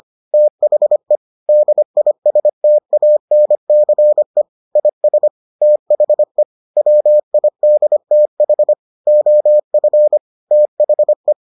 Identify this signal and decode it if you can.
Morse code, heard as a keyed tone, about 25 wpm, 600 Hz. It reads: THE DISTANCE IS THE WIDTH OF THE